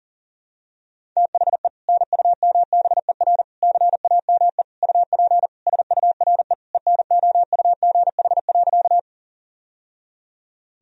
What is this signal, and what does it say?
Morse code, heard as a keyed tone, about 40 wpm, 700 Hz. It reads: THE NUMBER CAME UP SURE ENOUGH.
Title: Sentences from Top 1000 Words - Review Part 1 - 40wpm